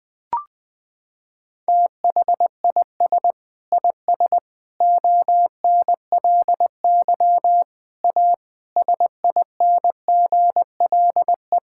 Morse code, heard as a keyed tone, about 20 wpm, 700 Hz. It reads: THIS IS ONLY A SINGLE
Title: Sentences from Top 1000 Words - Review Part 5 - 20wpm